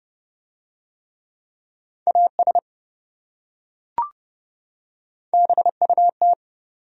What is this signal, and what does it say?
Morse code, heard as a keyed tone, about 27 wpm, 700 Hz. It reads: AS BUT